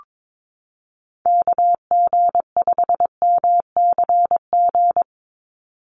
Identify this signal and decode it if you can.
Morse code, heard as a keyed tone, about 22 wpm, 700 Hz. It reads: KG5MCG
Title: USA Call Signs 22wpm